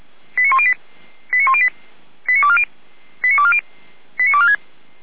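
Five-tone selective call (ZVEI), heard as quick runs of five tones, tones of about 80 ms. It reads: 89108 89108 89240 89240 89246
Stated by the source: ZVEI five-tone call received on an SDR, from the fire brigade Feuerwehr Orlamünde.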